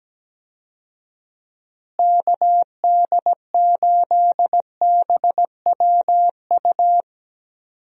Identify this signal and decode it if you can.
Morse code, heard as a keyed tone, about 17 wpm, 700 Hz. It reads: KD8BWU
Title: USA Call Signs - 6 Characters - Repeated 2x - 17wpm